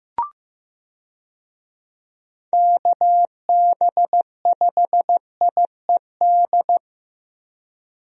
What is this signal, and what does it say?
Morse code, heard as a keyed tone, about 14 wpm, 700 Hz. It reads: KB5IED
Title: USA Call Signs - 6 Characters - Repeated 2x - 15wpm